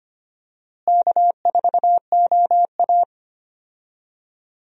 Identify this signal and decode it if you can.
Morse code, heard as a keyed tone, about 25 wpm, 700 Hz. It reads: K4OA
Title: USA Call Signs - 4 Characters - Repeated 2x - 25wpm